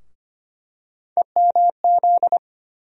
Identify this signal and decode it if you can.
Morse code, heard as a keyed tone, about 25 wpm, 700 Hz. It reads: EMZ
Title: USA Call Sign Postfixes - 3 Letters 25wpm